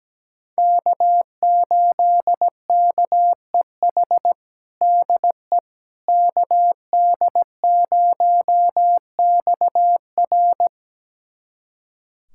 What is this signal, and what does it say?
Morse code, heard as a keyed tone, about 17 wpm, 700 Hz. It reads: K8KEH DE KD0XR
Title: USA Callsign DE Callsign - 5 Characters - 17wpm